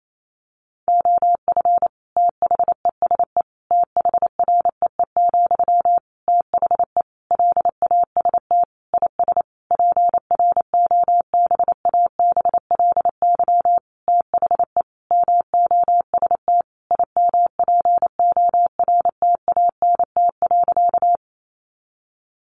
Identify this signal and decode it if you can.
Morse code, heard as a keyed tone, about 28 wpm, 700 Hz. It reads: OF THESE THREE, THE LAST IS PROBABLY THE MOST IMPORTANT.
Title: Sentences from Top 700 Words 28wpm